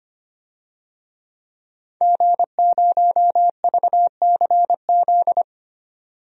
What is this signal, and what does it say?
Morse code, heard as a keyed tone, about 25 wpm, 700 Hz. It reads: G0VCZ
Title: DXCC-100 Call Signs - 5 Characters - 25wpm